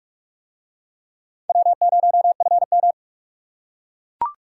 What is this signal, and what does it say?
Morse code, heard as a keyed tone, about 40 wpm, 700 Hz. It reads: W0FM